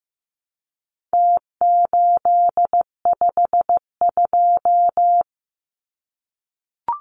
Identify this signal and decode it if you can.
Morse code, heard as a keyed tone, about 15 wpm, 700 Hz. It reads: T852
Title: Numbers - Contest Style - 4-Digit 15wpm